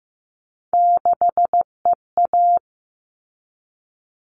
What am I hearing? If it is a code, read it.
Morse code, heard as a keyed tone, about 15 wpm, 700 Hz. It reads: 6EA
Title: All Cut Numbers - 3-Digit 15wpm